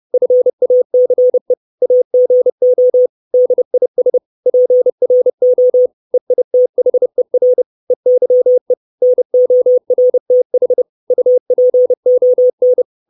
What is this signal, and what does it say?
Morse code, heard as a keyed tone, about 30 wpm, 500 Hz. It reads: FACE AGO DIS PRO EITHER EYE NORTH UPON
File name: 632 Words 30WPM